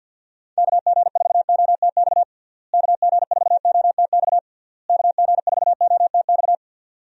Morse code, heard as a keyed tone, about 52 wpm, 700 Hz. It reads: KG4OTX KG4OTX KG4OTX
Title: USA Call Signs - 6 Characters - Repeated 3x - 50wpm